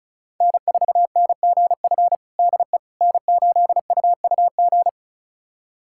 Morse code, heard as a keyed tone, about 34 wpm, 700 Hz. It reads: N4NGF DE N8UUG
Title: USA Callsign DE Callsign - 5 Characters - 35wpm